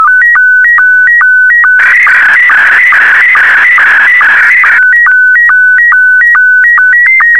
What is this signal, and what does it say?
SSTV (slow-scan television), Scottie 1.
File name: sstv / S1-VK4MSL